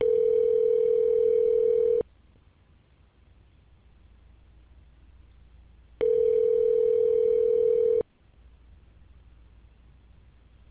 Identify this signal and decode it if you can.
Ringback tone (North America), 440 + 480 Hz.